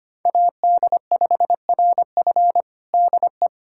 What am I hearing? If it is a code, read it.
Morse code, heard as a keyed tone, about 25 wpm, 700 Hz. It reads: AD5RF DE
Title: USA Callsign DE Callsign - 5 Characters - 25wpm